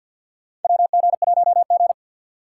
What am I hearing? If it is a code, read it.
Morse code, heard as a keyed tone, about 46 wpm, 700 Hz. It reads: WG1G
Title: USA Call Signs - 4 Characters - 3x Character Spacing - 50wpm